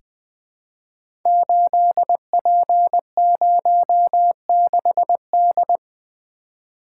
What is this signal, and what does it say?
Morse code, heard as a keyed tone, about 20 wpm, 700 Hz. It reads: 8P06D
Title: Code Groups - 5 Letter:Numbers - 20wpm